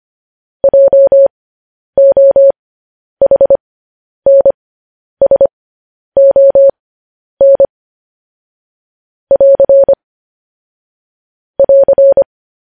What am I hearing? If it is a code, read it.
Morse code, heard as a keyed tone, about 25 wpm, spaced as for 10 wpm, 550 Hz. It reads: JOHNSON <AR> <AR>
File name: podcast / News headlines-1x-25x10